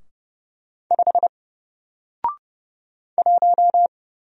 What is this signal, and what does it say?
Morse code, heard as a keyed tone, about 29 wpm, 700 Hz. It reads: 5 1
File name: Single-Letter-Number-30wpm